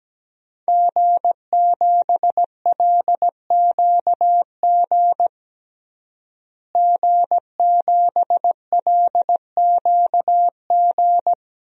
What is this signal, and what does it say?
Morse code, heard as a keyed tone, about 17 wpm, 700 Hz. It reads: G7LQG G7LQG
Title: DXCC-100 Call Signs - 5 Characters - Repeated 2x - 17wpm